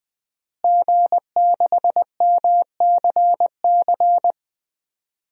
Morse code, heard as a keyed tone, about 21 wpm, 700 Hz. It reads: G6MCC
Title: DXCC-100 Call Signs - 5 Characters - Repeated 2x - 20wpm